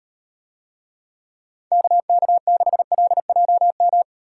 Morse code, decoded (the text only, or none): KK6LJM